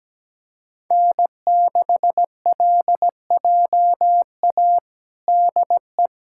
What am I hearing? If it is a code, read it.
Morse code, heard as a keyed tone, about 17 wpm, 700 Hz. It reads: N6LJA DE
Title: USA Callsign DE Callsign - 5 Characters - 17wpm